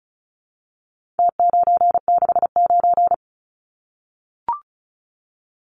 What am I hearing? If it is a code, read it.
Morse code, heard as a keyed tone, about 34 wpm, 700 Hz. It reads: T969
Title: Numbers - Contest Style - 4-Digit 35wpm